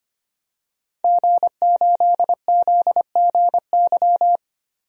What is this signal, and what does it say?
Morse code, heard as a keyed tone, about 25 wpm, 700 Hz. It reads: G8ZGY
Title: DXCC-100 Call Signs - 5 Characters - 25wpm